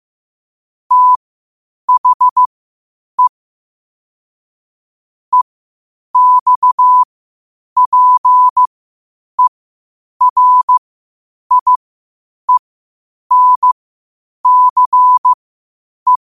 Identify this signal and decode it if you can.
Morse code, heard as a keyed tone, about 15 wpm, spaced as for 8 wpm, 1000 Hz. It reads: THE EXPERIENCE
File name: qotdfiles / 8wpm